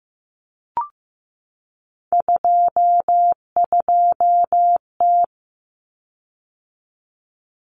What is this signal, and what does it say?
Morse code, heard as a keyed tone, about 15 wpm, 700 Hz. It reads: 22T
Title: Common Cut Numbers - 3-Digit 15wpm